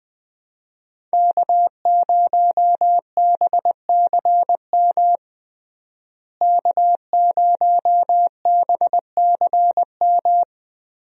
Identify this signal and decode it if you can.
Morse code, heard as a keyed tone, about 20 wpm, 700 Hz. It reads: K0BCM K0BCM
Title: USA Call Signs - 5 Characters - Repeated 2x - 20wpm